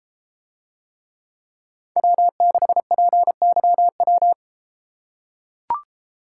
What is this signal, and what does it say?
Morse code, heard as a keyed tone, about 32 wpm, 700 Hz. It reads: W6PYW